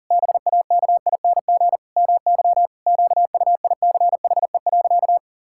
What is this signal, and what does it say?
Morse code, heard as a keyed tone, about 37 wpm, 700 Hz. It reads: BAKING MY QUICHE.
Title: Pangram Challenge v2 - 40wpm